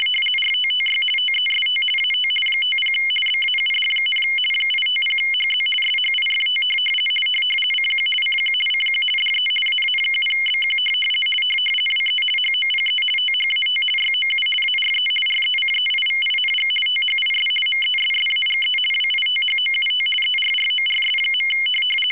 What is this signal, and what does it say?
RTTY, about 50 baud. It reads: RTTY TEST AT 50 BAUD, 850HZ SHIFT CQ CQ CQ DE AA7BQ WE HOPE THAT YOU ARE RECEIVING THIS MESSAGE WITHOUT ERRORS.
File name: rtty50